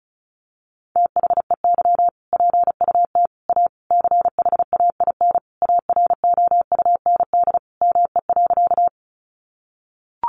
Morse code, heard as a keyed tone, about 34 wpm, 700 Hz. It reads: THEY PUT A CHAIN AROUND ME.
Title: Sentences from Top 1500 Words 35wpm